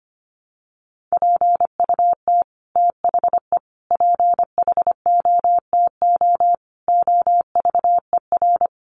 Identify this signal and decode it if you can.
Morse code, heard as a keyed tone, about 25 wpm, 700 Hz. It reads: PUT THE PHOTO OVER
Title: Sentences from Top 1200 Words 25wpm